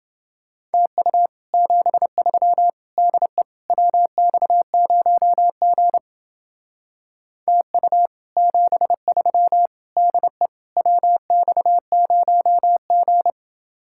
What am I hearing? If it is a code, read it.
Morse code, heard as a keyed tone, about 28 wpm, 700 Hz. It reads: TU 73 DE WX0G TU 73 DE WX0G